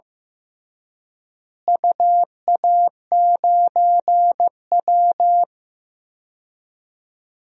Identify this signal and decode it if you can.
Morse code, heard as a keyed tone, about 15 wpm, 700 Hz. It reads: UA9W
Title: DXCC-100 Call Signs - 4 Characters - Repeated 2x - 15wpm